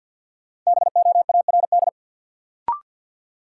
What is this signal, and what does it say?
Morse code, heard as a keyed tone, about 52 wpm, 700 Hz. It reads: BOARD
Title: Sentences from Top 700 Words - New Word Drill - 50wpm